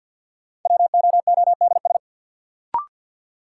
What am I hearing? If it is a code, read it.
Morse code, heard as a keyed tone, about 52 wpm, 700 Hz. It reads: WOODS
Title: Sentences from Top 1000 Words - Review Part 2 - New Word Drill - 50wpm